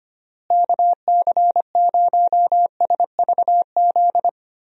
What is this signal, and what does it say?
Morse code, heard as a keyed tone, about 25 wpm, 700 Hz. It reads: KC0SVZ
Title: USA Call Signs - 6 Characters - Repeated 3x - 25wpm